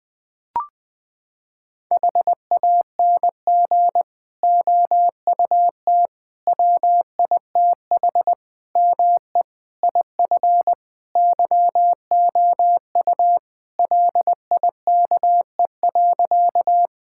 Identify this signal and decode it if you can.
Morse code, heard as a keyed tone, about 20 wpm, 700 Hz. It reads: HANG OUT WITH ME IF YOU LIKE.
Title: Sentences from Top 1600 Words 20wpm